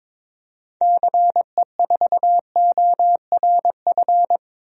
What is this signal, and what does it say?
Morse code, heard as a keyed tone, about 21 wpm, 700 Hz. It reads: CE4ORF
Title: DXCC-100 Call Signs - 6 Characters - Repeated 3x - 22wpm